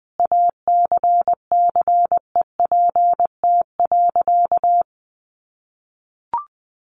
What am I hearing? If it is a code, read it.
Morse code, heard as a keyed tone, about 20 wpm, 700 Hz. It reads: ACCEPT.
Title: Sentences from Top 1300 Words 20wpm